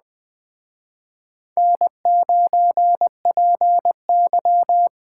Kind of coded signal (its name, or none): Morse code